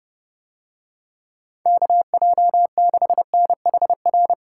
Morse code, heard as a keyed tone, about 30 wpm, 700 Hz. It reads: KJ6NHR